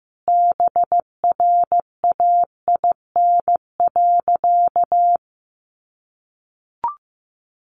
Morse code, heard as a keyed tone, about 15 wpm, 700 Hz. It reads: BRAIN.
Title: Sentences from Top 1100 Words 15wpm